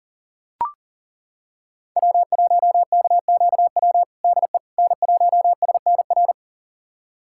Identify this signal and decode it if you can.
Morse code, heard as a keyed tone, about 40 wpm, 700 Hz. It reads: W1KQW DE N1SNR
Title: USA Callsign DE Callsign - 5 Characters - 40wpm